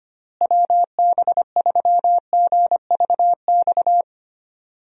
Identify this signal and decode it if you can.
Morse code, heard as a keyed tone, about 25 wpm, 700 Hz. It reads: WB3GVX
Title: USA Call Signs - 6 Characters - Repeated 2x - 25wpm